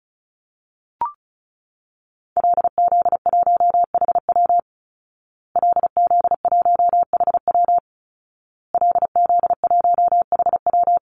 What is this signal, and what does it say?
Morse code, heard as a keyed tone, about 34 wpm, 700 Hz. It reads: LZ1HW LZ1HW LZ1HW
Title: Call Signs Repeated 2x - 35wpm